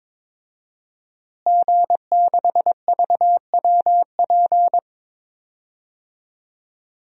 Morse code, heard as a keyed tone, about 22 wpm, 700 Hz. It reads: G6VWP
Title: DXCC-100 Call Signs - 5 Characters - Repeated 2x - 22wpm